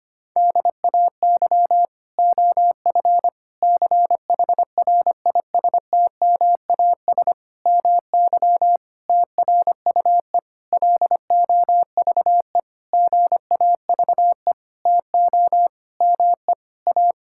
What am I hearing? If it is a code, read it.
Morse code, heard as a keyed tone, about 25 wpm, 700 Hz. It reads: DAY OF CHRISTMAS MY TRUE LOVE GAVE TO ME A
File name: practice / 25wpm / Sentences with 9-Word Phrases - 25wpm